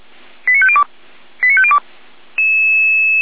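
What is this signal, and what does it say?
Five-tone selective call (ZVEI), heard as quick runs of five tones, tones of about 70 ms. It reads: 89502 89502
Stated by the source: ZVEI five-tone selective call received by SDR, recorded on the Feuerwehr Orlamünde server.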